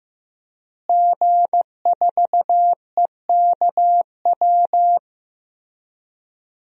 Morse code, heard as a keyed tone, about 15 wpm, 700 Hz. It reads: G4EKW
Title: DXCC-100 Call Signs - 5 Characters - Repeated 3x - 15wpm